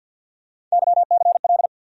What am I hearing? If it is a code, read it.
Morse code, heard as a keyed tone, about 47 wpm, 700 Hz. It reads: YCL
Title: USA Call Sign Postfixes - 3 Letters 50wpm